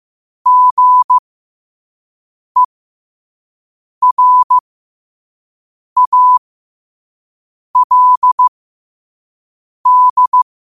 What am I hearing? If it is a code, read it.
Morse code, heard as a keyed tone, about 15 wpm, spaced as for 5 wpm, 1000 Hz. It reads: GERALD
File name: qotdfiles / 5wpm